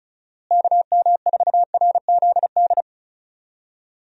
Morse code, heard as a keyed tone, about 36 wpm, 700 Hz. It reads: KM4RZD
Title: USA Call Signs - 6 Characters - Repeated 2x - 35wpm